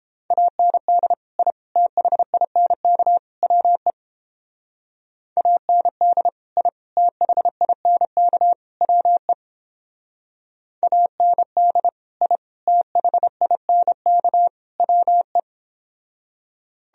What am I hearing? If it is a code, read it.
Morse code, heard as a keyed tone, about 31 wpm, 700 Hz. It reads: AND I THINK WE AND I THINK WE AND I THINK WE